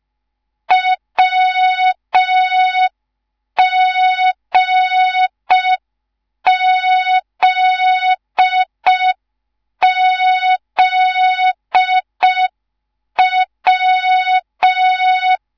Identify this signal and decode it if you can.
Morse code, heard as a keyed tone, about 5 wpm, 3700 Hz. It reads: WGZZW